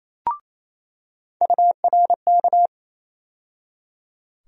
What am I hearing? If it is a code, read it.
Morse code, heard as a keyed tone, about 27 wpm, 700 Hz. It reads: URK
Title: USA Call Sign Postfixes - 3 Letters 28wpm